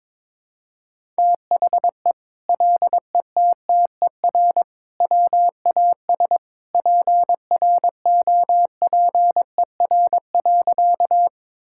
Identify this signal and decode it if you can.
Morse code, heard as a keyed tone, about 22 wpm, 700 Hz. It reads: THE LETTER WAS PROPER.
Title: Sentences from Top 1000 Words - Review Part 5 - 22wpm